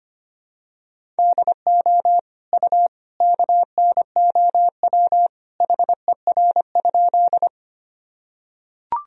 Morse code, heard as a keyed tone, about 25 wpm, 700 Hz. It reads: DO U KNOW HER?
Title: Sentences from Top 100 Words and Abbreviations 25wpm